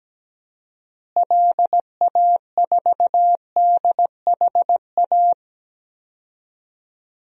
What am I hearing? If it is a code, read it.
Morse code, heard as a keyed tone, about 17 wpm, 700 Hz. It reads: LA4DHA
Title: DXCC-100 Call Signs - 6 Characters - Repeated 3x - 17wpm